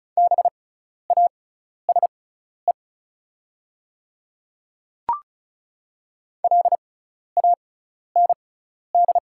Morse code, heard as a keyed tone, about 36 wpm, spaced as for 12 wpm, 690 Hz. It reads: BASE LAND
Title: 6x Character Spacing - Top 300 Words - 35wpm